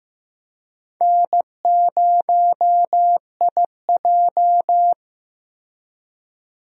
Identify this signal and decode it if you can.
Morse code, heard as a keyed tone, about 15 wpm, 700 Hz. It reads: N0IJ
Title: USA Call Signs - 4 Characters - Repeated 3x - 15wpm